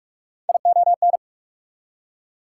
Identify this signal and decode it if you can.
Morse code, heard as a keyed tone, about 57 wpm, 690 Hz. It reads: ION